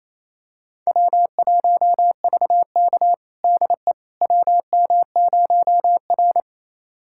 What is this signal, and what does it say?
Morse code, heard as a keyed tone, about 28 wpm, 700 Hz. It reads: W1VK DE WM0R
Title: USA Callsign DE Callsign - 4 Characters - 28wpm